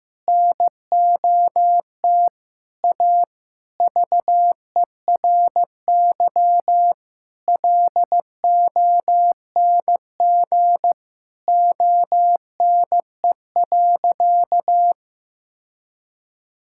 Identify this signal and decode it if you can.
Morse code, heard as a keyed tone, about 16 wpm, 700 Hz. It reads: NOT A VERY LONG ONE.
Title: Sentences from Top 1000 Words - Review Part 5 - 15wpm